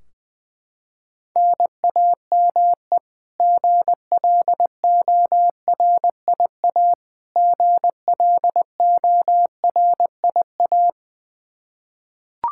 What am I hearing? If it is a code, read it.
Morse code, heard as a keyed tone, about 20 wpm, 700 Hz. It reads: NAME GLORIA GLORIA
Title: K1USN SST QSO Practice 20wpm